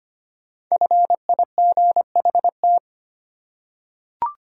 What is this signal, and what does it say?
Morse code, heard as a keyed tone, about 25 wpm, 700 Hz. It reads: FIGHT
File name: 5 Letter Words - First 3 Letters Given - 25wpm